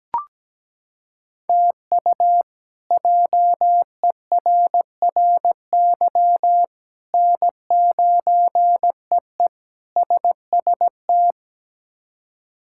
Morse code, heard as a keyed tone, about 17 wpm, 700 Hz. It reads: TU JERRY N9EE SST
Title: K1USN SST QSO Practice - 17wpm